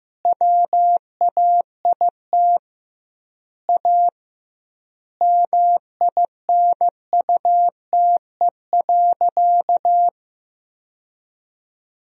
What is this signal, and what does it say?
Morse code, heard as a keyed tone, about 15 wpm, 700 Hz. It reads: WAIT A MINUTE.
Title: 2x Word-Spacing - Sentences from Top 500 Words - 15wpm